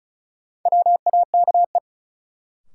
Morse code, heard as a keyed tone, about 36 wpm, 700 Hz. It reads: WAKE
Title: Sets of 4 Words - New Word Drill - 35wpm